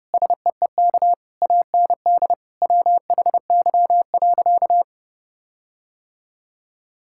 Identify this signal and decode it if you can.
Morse code, heard as a keyed tone, about 32 wpm, 700 Hz. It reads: SEEK AND WHY.
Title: Sentences from Top 1600 Words 30wpm